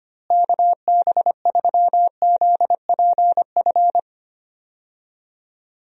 Morse code, heard as a keyed tone, about 25 wpm, 700 Hz. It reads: KB3ZPF
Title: USA Call Signs - 6 Characters - Repeated 3x - 25wpm